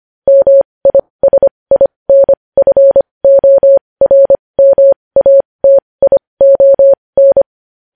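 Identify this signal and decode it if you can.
Morse code, heard as a keyed tone, about 25 wpm, spaced as for 20 wpm, 550 Hz. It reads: MISINFORMATION
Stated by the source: Morse code news headlines sent at 20.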